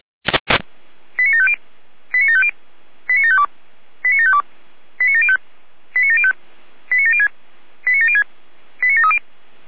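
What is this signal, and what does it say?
Five-tone selective call (ZVEI), heard as quick runs of five tones, tones of about 70 ms. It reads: 89750 89750 89752 89752 89705 89705 89706 89706 89730
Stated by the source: ZVEI five-tone call received by SDR.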